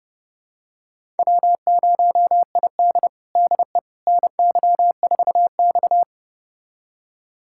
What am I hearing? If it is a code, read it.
Morse code, heard as a keyed tone, about 30 wpm, 700 Hz. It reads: W0ID DE NY4X
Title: USA Callsign DE Callsign - 4 Characters - 30wpm